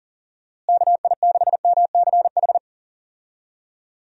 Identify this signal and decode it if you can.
Morse code, heard as a keyed tone, about 40 wpm, 700 Hz. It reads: KI6MCH
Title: USA Call Signs - 6 Characters - Repeated 2x - 40wpm